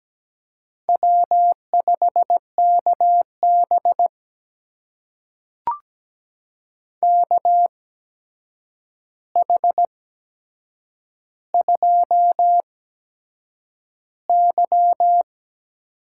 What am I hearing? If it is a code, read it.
Morse code, heard as a keyed tone, about 17 wpm, spaced as for 7 wpm, 700 Hz. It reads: W5KB KH2Y
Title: USA Call Signs - 4 Characters - 8x Character Spacing - 17wpm